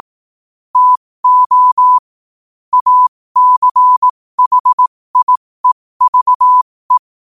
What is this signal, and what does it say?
Morse code, heard as a keyed tone, about 18 wpm, spaced as for 15 wpm, 1000 Hz. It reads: TO ACHIEVE
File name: qotdfiles / 15wpm